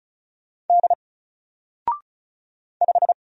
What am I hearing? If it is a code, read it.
Morse code, heard as a keyed tone, about 36 wpm, 700 Hz. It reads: D 5
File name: Single-Letter-Number-35wpm